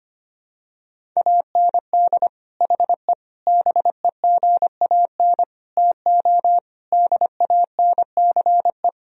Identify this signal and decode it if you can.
Morse code, heard as a keyed tone, about 25 wpm, 700 Hz. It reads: AND HE BEGAN TO DANCE
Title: Sentences from Top 1000 Words - Review Part 4 - 25wpm